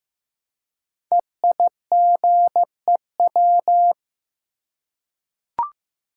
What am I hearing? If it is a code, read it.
Morse code, heard as a keyed tone, about 15 wpm, 700 Hz. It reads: EIGEW
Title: Code Groups - 5 Letters - 15wpm